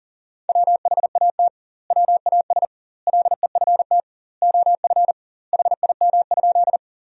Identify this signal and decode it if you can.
Morse code, heard as a keyed tone, about 38 wpm, 700 Hz. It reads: WHAT WAS LEFT OF HIM?
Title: 2x Word-Spacing - Sentences from Top 300 Words - 40wpm